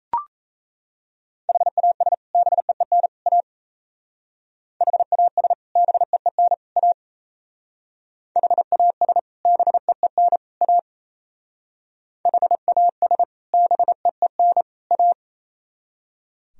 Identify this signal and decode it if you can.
Morse code, heard as a keyed tone, about 36 wpm, 700 Hz. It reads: HAS BEEN A HAS BEEN A HAS BEEN A HAS BEEN A